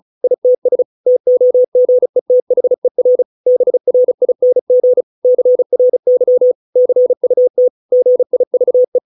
Morse code, heard as a keyed tone, about 34 wpm, 500 Hz. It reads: ITS TOGETHER BRING CRY CUT GIVE